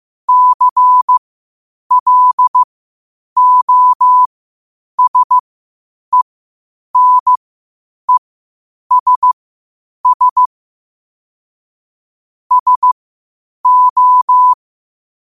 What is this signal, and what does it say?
Morse code, heard as a keyed tone, about 15 wpm, spaced as for 8 wpm, 1000 Hz. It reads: CLOSENESS SO
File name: qotdfiles / 8wpm